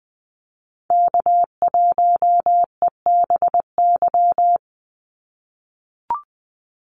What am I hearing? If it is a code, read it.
Morse code, heard as a keyed tone, about 20 wpm, 700 Hz. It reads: K1EBY
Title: Call Signs - 20wpm